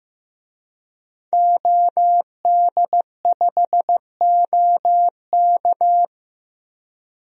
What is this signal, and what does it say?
Morse code, heard as a keyed tone, about 15 wpm, 700 Hz. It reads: OD5OK
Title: DXCC-100 Call Signs - 5 Characters - 15wpm